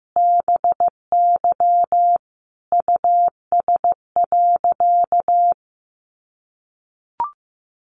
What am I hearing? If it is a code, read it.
Morse code, heard as a keyed tone, about 15 wpm, 700 Hz. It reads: BY US.
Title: A Sound Of Thunder 15wpm